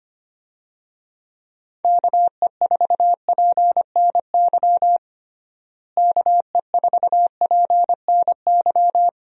Morse code, heard as a keyed tone, about 25 wpm, 700 Hz. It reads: KE4PNY KE4PNY
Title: USA Call Signs - 6 Characters - Repeated 3x - 25wpm